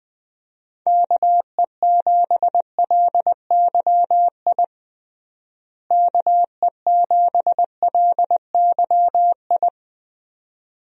Morse code, heard as a keyed tone, about 20 wpm, 700 Hz. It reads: KE7LYI KE7LYI
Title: USA Call Signs - 6 Characters - Repeated 2x - 20wpm